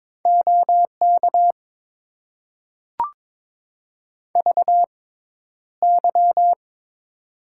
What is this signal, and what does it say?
Morse code, heard as a keyed tone, about 22 wpm, spaced as for 11 wpm, 700 Hz. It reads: OK VY
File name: Top 90 QSO Elements - 6x Character Spacing - 22wpm